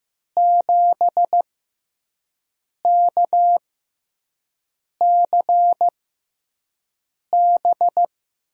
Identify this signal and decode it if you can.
Morse code, heard as a keyed tone, about 15 wpm, spaced as for 5 wpm, 700 Hz. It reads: 7KCB
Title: USA Call Signs - 6x Character Spacing - 15wpm